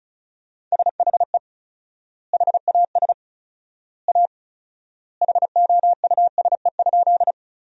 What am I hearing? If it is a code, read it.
Morse code, heard as a keyed tone, about 34 wpm, 700 Hz. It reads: SHE HAS A HOUSE?
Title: 4x Word-Spacing - Sentences from Top 200 Words - 35wpm